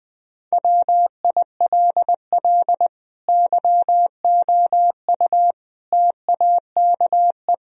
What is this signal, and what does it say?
Morse code, heard as a keyed tone, about 20 wpm, 700 Hz. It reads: WILL YOU TAKE
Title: Sentences from Top 1000 Words - Review Part 3 - 20wpm